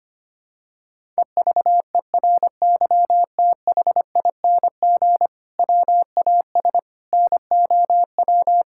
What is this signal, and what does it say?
Morse code, heard as a keyed tone, about 25 wpm, 700 Hz. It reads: EVERYTHING WAS NOW
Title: Sentences from Top 1000 Words - Review Part 5 - 25wpm